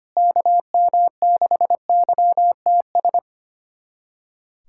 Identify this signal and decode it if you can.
Morse code, heard as a keyed tone, about 25 wpm, 700 Hz. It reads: KM6YTS
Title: USA Callsign DE Callsign - 6 Characters - 25wpm